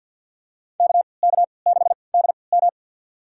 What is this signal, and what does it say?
Morse code, heard as a keyed tone, about 47 wpm, spaced as for 28 wpm, 700 Hz. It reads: KK6DM
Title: USA Call Signs - 5 Characters - 3x Character Spacing - 50wpm